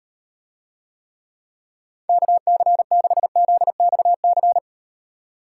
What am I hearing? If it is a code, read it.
Morse code, heard as a keyed tone, about 38 wpm, 700 Hz. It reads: KC6ZXC